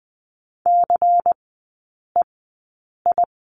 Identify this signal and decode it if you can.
Morse code, heard as a keyed tone, about 20 wpm, spaced as for 9 wpm, 700 Hz. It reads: CEI
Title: 093 Lesson 16e - Letter c - Three Characters 20wpm